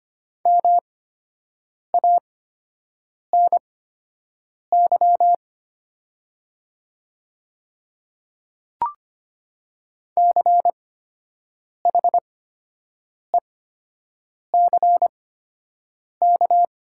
Morse code, heard as a keyed tone, about 25 wpm, spaced as for 7 wpm, 700 Hz. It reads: MANY CHECK
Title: 8x Character Spacing - Top 500 Words - 25wpm